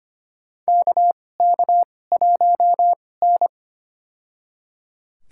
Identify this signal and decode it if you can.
Morse code, heard as a keyed tone, about 25 wpm, spaced as for 18 wpm, 700 Hz. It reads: KK1N
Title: USA Call Signs - 2x Character Spacing - 25wpm